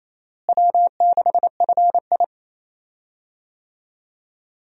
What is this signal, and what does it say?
Morse code, heard as a keyed tone, about 28 wpm, 700 Hz. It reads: W6FI
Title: USA Call Signs - 4 Characters - Repeated 3x - 28wpm